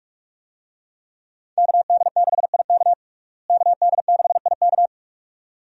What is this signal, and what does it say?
Morse code, heard as a keyed tone, about 43 wpm, 700 Hz. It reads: KD6IK KD6IK
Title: USA Call Signs - 5 Characters - Repeated 2x - 45wpm